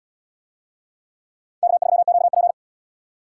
Morse code, heard as a keyed tone, about 77 wpm, 700 Hz. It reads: K3QP